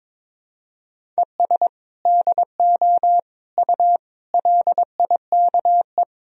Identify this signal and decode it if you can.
Morse code, heard as a keyed tone, about 22 wpm, 700 Hz. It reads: ES DO U LIKE
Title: Sentences from Top 100 Words and Abbreviations 22wpm